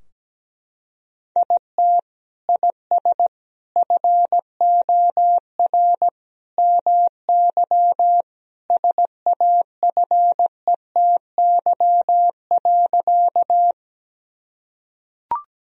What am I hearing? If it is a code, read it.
Morse code, heard as a keyed tone, about 17 wpm, 700 Hz. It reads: IT IS FOR MY SAFETY.
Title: Sentences from Top 1000 Words - Review Part 4 - 17wpm